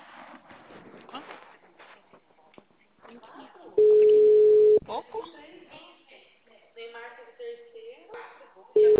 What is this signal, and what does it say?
Ringback tone (Europe, 425 Hz), with speech.
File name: out- -2005- - - .125